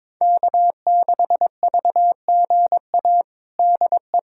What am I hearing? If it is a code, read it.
Morse code, heard as a keyed tone, about 22 wpm, 700 Hz. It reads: K6VGA DE
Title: USA Callsign DE Callsign 22wpm